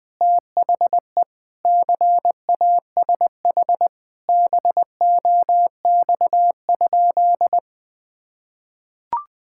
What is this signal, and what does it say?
Morse code, heard as a keyed tone, about 20 wpm, 700 Hz. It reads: THE CASH BOX?
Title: Sentences from Top 1000 Words - Review Part 5 - 20wpm